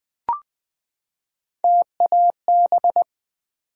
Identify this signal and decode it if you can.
Morse code, heard as a keyed tone, about 20 wpm, 700 Hz. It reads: TAB
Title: Build A Word - 3 Letters At A Time - 6-Letter Words - 20wpm